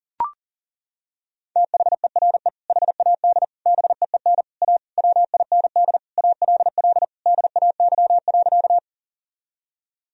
Morse code, heard as a keyed tone, about 40 wpm, 700 Hz. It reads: THERE HAD BEEN A WIND ALL DAY.
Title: Sentences from Top 1000 Words - Review Part 2 - 40wpm